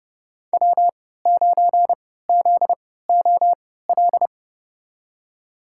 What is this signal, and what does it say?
Morse code, heard as a keyed tone, about 30 wpm, spaced as for 17 wpm, 700 Hz. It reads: W9ZOL
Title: USA Call Signs - 5 Characters - 3x Character Spacing - 30wpm